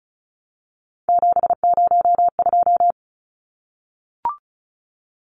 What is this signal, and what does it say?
Morse code, heard as a keyed tone, about 34 wpm, 700 Hz. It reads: 702
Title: Numbers - 3-Digit 35wpm